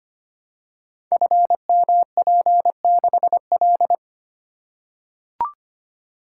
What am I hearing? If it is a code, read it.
Morse code, heard as a keyed tone, about 25 wpm, 700 Hz. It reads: FMP6L
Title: Code Groups - 5 Letter:Numbers - 25wpm